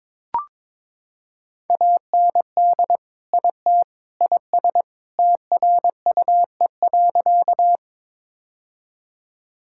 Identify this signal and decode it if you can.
Morse code, heard as a keyed tone, about 22 wpm, 700 Hz. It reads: AND IT IS TRUE.
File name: practice / 22wpm / Sentences with 3-Word Phrases - 22wpm